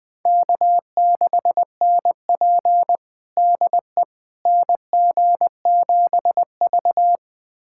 Morse code, heard as a keyed tone, about 20 wpm, 700 Hz. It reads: K6NP DE NG7V
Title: USA Callsign DE Callsign - 4 Characters - 20wpm